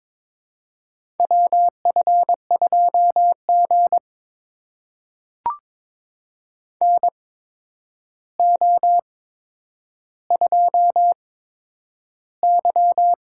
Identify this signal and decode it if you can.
Morse code, heard as a keyed tone, about 22 wpm, spaced as for 9 wpm, 700 Hz. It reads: WF2G NO2Y
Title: USA Call Signs - 4 Characters - 8x Character Spacing - 22wpm